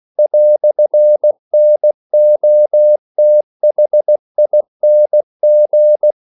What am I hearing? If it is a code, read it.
Morse code, heard as a keyed tone, about 16 wpm, 600 Hz. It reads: "NOTHING
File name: CW / Practice / EnglishQuotations0116